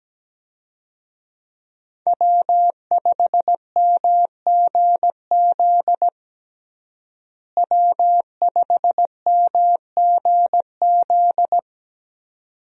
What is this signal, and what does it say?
Morse code, heard as a keyed tone, about 17 wpm, 700 Hz. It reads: W5MGZ W5MGZ
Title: USA Call Signs - 5 Characters - Repeated 2x - 17wpm